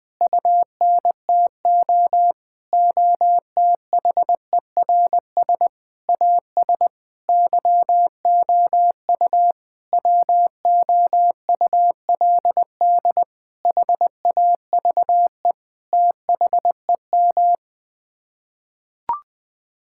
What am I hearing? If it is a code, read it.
Morse code, heard as a keyed tone, about 21 wpm, 700 Hz. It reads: UNTO OTHERS AS YOU WOULD HAVE THEM
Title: Sets of 8 Words - 20wpm